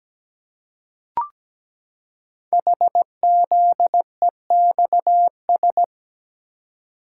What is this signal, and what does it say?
Morse code, heard as a keyed tone, about 17 wpm, 700 Hz. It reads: HZEXS
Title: Code Groups - 5 Letters - 17wpm